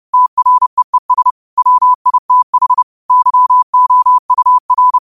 Morse code, heard as a keyed tone, about 30 wpm, 1000 Hz. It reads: TREES WITH YOUR